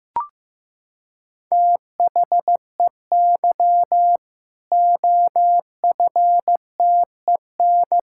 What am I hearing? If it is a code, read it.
Morse code, heard as a keyed tone, about 16 wpm, 700 Hz. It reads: THEY OFTEN
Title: Sentences from Top 1600 Words 15wpm